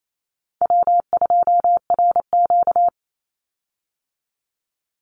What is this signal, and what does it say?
Morse code, heard as a keyed tone, about 28 wpm, 700 Hz. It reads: W2RQ
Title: Call Signs Repeated 2x - 28wpm